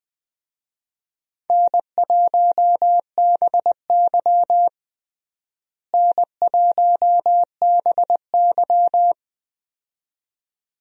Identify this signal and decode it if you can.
Morse code, heard as a keyed tone, about 20 wpm, 700 Hz. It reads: N1BY N1BY
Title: USA Call Signs - 4 Characters - Repeated 2x - 20wpm